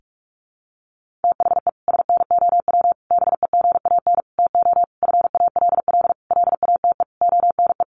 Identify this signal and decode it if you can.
Morse code, heard as a keyed tone, about 44 wpm, 700 Hz. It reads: THE SNOW BEGAN TO FALL LATE ONE
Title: Sentences from Top 1500 Words 45wpm